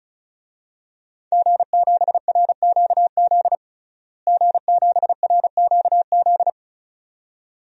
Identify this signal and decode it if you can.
Morse code, heard as a keyed tone, about 34 wpm, 700 Hz. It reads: G7RQZ G7RQZ
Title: DXCC-100 Call Signs - 5 Characters - Repeated 2x - 35wpm